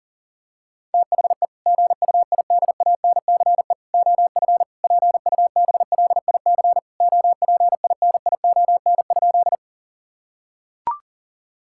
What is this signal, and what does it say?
Morse code, heard as a keyed tone, about 40 wpm, 700 Hz. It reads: THE GUIDANCE OF PUBLIC OPINION?
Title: Sentences from Top 1600 Words 40wpm